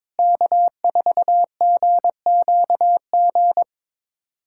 Morse code, heard as a keyed tone, about 22 wpm, 700 Hz. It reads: K4GQG
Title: USA Callsign DE Callsign - 5 Characters - 22wpm